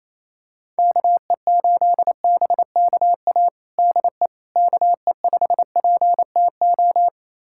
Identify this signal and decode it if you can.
Morse code, heard as a keyed tone, about 28 wpm, 700 Hz. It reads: KE8BKA DE KE5PTO
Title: USA Callsign DE Callsign - 6 Characters - 28wpm